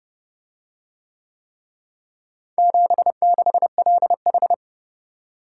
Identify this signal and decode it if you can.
Morse code, heard as a keyed tone, about 29 wpm, 700 Hz. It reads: 76LH